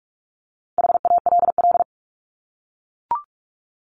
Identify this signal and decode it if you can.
Morse code, heard as a keyed tone, about 44 wpm, 700 Hz. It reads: HALL
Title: Top 1751-2000 Words 45wpm